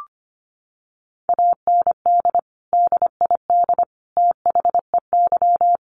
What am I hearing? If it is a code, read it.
Morse code, heard as a keyed tone, about 25 wpm, 700 Hz. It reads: AND DID THEY
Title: Sentences from Top 1500 Words 25wpm